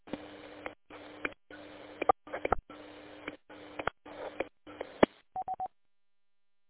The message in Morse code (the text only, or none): none